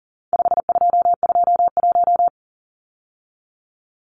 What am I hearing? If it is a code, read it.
Morse code, heard as a keyed tone, about 40 wpm, 700 Hz. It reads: 5221
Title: Numbers - 4-Digit 40wpm